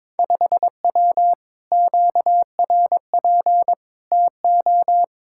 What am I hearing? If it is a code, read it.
Morse code, heard as a keyed tone, about 22 wpm, 700 Hz. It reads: 5W QRP TO
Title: Speed-Racing Common QSO Phrases Encore - 22wpm